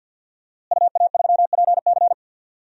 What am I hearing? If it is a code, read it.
Morse code, heard as a keyed tone, about 52 wpm, 700 Hz. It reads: UA3PC
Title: DXCC-100 Call Signs - 5 Characters - Repeated 3x - 50wpm